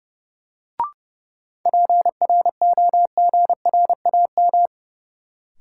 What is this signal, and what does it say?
Morse code, heard as a keyed tone, about 30 wpm, 700 Hz. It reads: PROGRAM
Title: Sentences from Top 1000 Words - Review Part 4 - New Word Drill - 30wpm